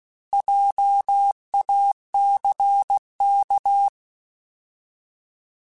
Morse code, heard as a keyed tone, about 16 wpm, 800 Hz. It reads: JACK
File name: Alphabetical / Alphabetical mp3s / Js @15